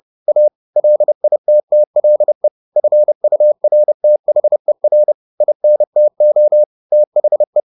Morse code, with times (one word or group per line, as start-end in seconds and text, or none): 0.3-0.5 A
0.8-2.5 LITTLE
2.8-5.1 FURTHER
5.4-6.6 INTO
6.9-7.6 THE